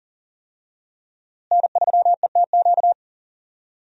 Morse code, heard as a keyed tone, about 40 wpm, 700 Hz. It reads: N3ETQ
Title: USA Call Signs - 5 Characters - Repeated 2x - 40wpm